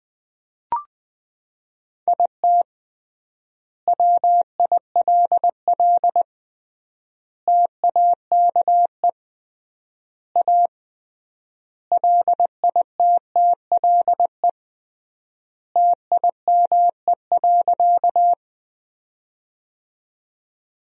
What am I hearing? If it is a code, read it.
Morse code, heard as a keyed tone, about 21 wpm, 700 Hz. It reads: IT WILL TAKE A LITTLE TIME.
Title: 3x Word-Spacing - Sentences from Top 200 Words - 20wpm